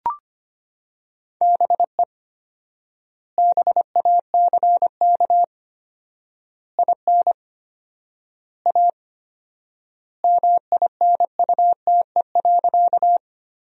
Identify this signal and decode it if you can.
Morse code, heard as a keyed tone, about 25 wpm, 700 Hz. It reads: BE BACK IN A MINUTE.
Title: 4x Word-Spacing - Sentences from Top 500 Words - 25wpm